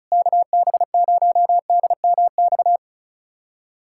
Morse code, heard as a keyed tone, about 36 wpm, 700 Hz. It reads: KB0DMX
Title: USA Callsign DE Callsign 35wpm